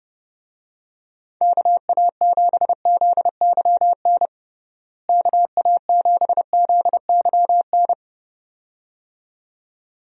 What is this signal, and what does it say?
Morse code, heard as a keyed tone, about 32 wpm, 700 Hz. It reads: KA7ZYN KA7ZYN
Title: USA Call Signs - 6 Characters - Repeated 2x - 30wpm